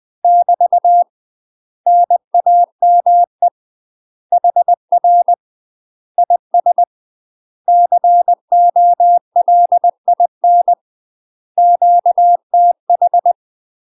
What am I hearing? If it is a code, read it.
Morse code, heard as a keyed tone, about 20 wpm, 700 Hz. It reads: <BT> NAME HR IS COLIN QTH
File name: UK MASTER QSOs TRAINING / 20WPM / UK MASTER QSOs 0048